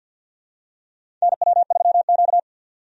Morse code, heard as a keyed tone, about 52 wpm, 700 Hz. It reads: NW3Q
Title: USA Call Signs - 4 Characters - 50wpm